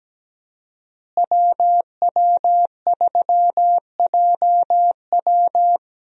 Morse code, heard as a keyed tone, about 17 wpm, 700 Hz. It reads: WW3JW
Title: USA Call Signs - 5 Characters - Repeated 3x - 17wpm